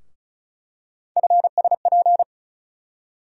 Morse code, heard as a keyed tone, about 36 wpm, 700 Hz. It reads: FSP
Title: USA Call Sign Postfixes - 3 Letters 35wpm